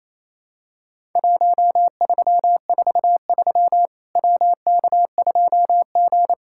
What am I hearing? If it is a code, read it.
Morse code, heard as a keyed tone, about 28 wpm, 700 Hz. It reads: 1343 WK2G
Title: Worked All Europe DX Contest - QTC Exchange - 28wpm